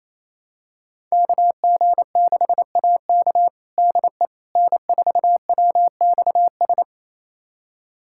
Morse code, heard as a keyed tone, about 27 wpm, 700 Hz. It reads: KG6AK DE N4WXS
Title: USA Callsign DE Callsign - 5 Characters - 28wpm